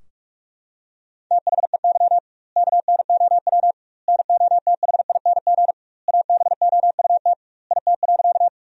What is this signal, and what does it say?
Morse code, heard as a keyed tone, about 44 wpm, 700 Hz. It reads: THEY KNOW NOTHING ABOUT IT.
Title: 2x Word-Spacing - Sentences from Top 500 Words - 45wpm